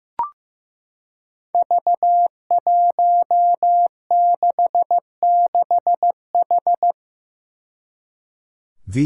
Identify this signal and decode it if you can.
Morse code, heard as a keyed tone, about 15 wpm, 700 Hz. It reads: V166H
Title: Five Challenging Letter-Number Combinations 15wpm